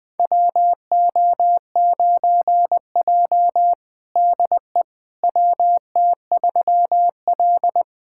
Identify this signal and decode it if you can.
Morse code, heard as a keyed tone, about 20 wpm, 700 Hz. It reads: WO9J DE WT3L
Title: USA Callsign DE Callsign - 4 Characters - 20wpm